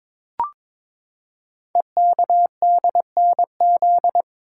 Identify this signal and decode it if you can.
Morse code, heard as a keyed tone, about 22 wpm, 700 Hz. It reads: EKDNZ
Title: Code Groups - 5 Letters - 22wpm